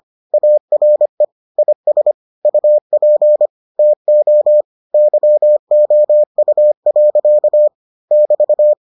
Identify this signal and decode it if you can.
Morse code, heard as a keyed tone, about 25 wpm, 600 Hz. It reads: ARE IS UP TO YOU. <BT>